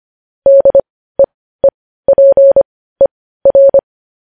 Morse code, heard as a keyed tone, about 25 wpm, spaced as for 15 wpm, 550 Hz. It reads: DEEPER
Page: news headlines in morse code at 15 wpm